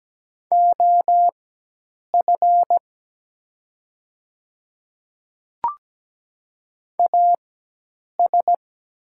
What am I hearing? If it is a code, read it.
Morse code, heard as a keyed tone, about 17 wpm, spaced as for 8 wpm, 700 Hz. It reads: OF AS
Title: 4x Character Spacing - Top 100 Words - 17wpm